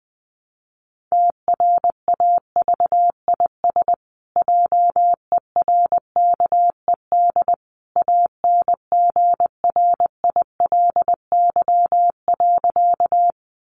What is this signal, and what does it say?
Morse code, heard as a keyed tone, about 20 wpm, 700 Hz. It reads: TRAVIS JERKED ANGRILY.
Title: A Sound Of Thunder 20wpm